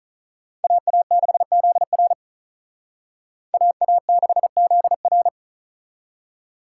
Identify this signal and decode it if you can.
Morse code, heard as a keyed tone, about 44 wpm, 700 Hz. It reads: AA6ZR AA6ZR
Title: Speed Racing - USA Call Signs 35wpm